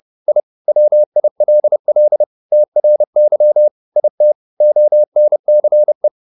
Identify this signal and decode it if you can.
Morse code, heard as a keyed tone, about 30 wpm, 600 Hz. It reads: I WILL TRY IT ONCE